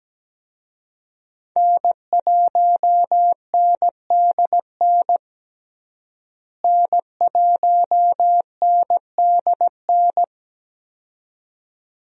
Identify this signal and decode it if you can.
Morse code, heard as a keyed tone, about 17 wpm, 700 Hz. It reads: N1NDN N1NDN
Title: USA Call Signs - 5 Characters - Repeated 2x - 17wpm